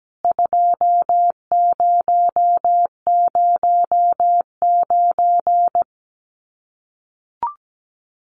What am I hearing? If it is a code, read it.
Morse code, heard as a keyed tone, about 17 wpm, 700 Hz. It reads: 2009
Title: CWT Exchange 17wpm